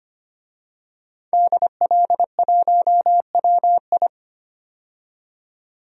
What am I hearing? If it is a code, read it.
Morse code, heard as a keyed tone, about 25 wpm, 700 Hz. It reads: DL1WI
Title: DXCC-100 Call Signs - 5 Characters - 25wpm